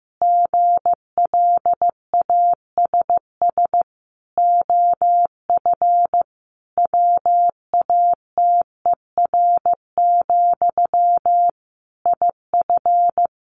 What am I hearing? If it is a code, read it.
Morse code, heard as a keyed tone, about 15 wpm, 700 Hz. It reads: GLASS OF WATER, IF
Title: Sentences from Top 1500 Words 15wpm